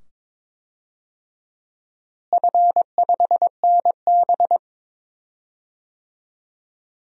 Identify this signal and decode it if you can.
Morse code, heard as a keyed tone, about 22 wpm, 700 Hz. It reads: F5NB
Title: DXCC-100 Call Signs - 4 Characters - Repeated 2x - 22wpm